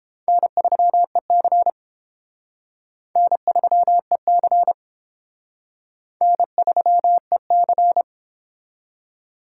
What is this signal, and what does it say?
Morse code, heard as a keyed tone, about 30 wpm, 700 Hz. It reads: N3EC N3EC N3EC